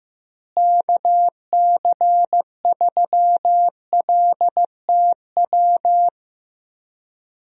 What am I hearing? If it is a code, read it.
Morse code, heard as a keyed tone, about 15 wpm, 700 Hz. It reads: KC3LTW
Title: USA Call Signs - 6 Characters - Repeated 3x - 15wpm